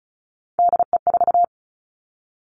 Morse code, heard as a keyed tone, about 36 wpm, 700 Hz. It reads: DE4
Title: Three Challenging Letter-Number Combinations 35wpm